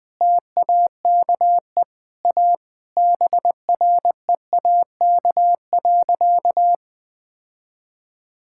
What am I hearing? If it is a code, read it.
Morse code, heard as a keyed tone, about 20 wpm, 700 Hz. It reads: TAKE A BREAK.
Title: Sentences with 4-Word Phrases - 20wpm